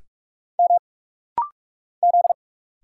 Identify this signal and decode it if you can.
Morse code, heard as a keyed tone, about 44 wpm, 700 Hz. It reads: M Z
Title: Single Letter-Number Alt - Rapid-Fire 45wpm